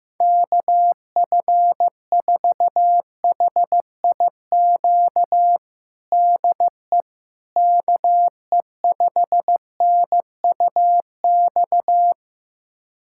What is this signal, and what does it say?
Morse code, heard as a keyed tone, about 15 wpm, 700 Hz. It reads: KF4HIQ DE KE5NUX
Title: USA Callsign DE Callsign 15wpm